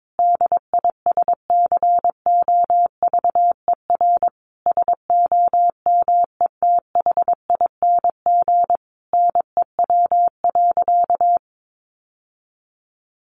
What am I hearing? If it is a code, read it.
Morse code, heard as a keyed tone, about 22 wpm, 700 Hz. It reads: DISCOVER SOMETHING NEW.
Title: Sentences from Top 1100 Words 22wpm